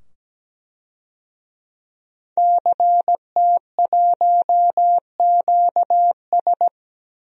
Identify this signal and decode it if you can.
Morse code, heard as a keyed tone, about 17 wpm, 700 Hz. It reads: CT1QS
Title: DXCC-100 Call Signs - 5 Characters - 17wpm